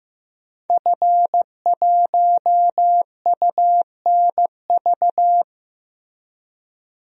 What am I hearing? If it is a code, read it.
Morse code, heard as a keyed tone, about 15 wpm, 700 Hz. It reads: F1UNV
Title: DXCC-100 Call Signs - 5 Characters - Repeated 3x - 15wpm